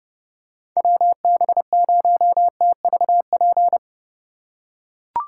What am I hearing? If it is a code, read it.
Morse code, heard as a keyed tone, about 30 wpm, 700 Hz. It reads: WB0TVP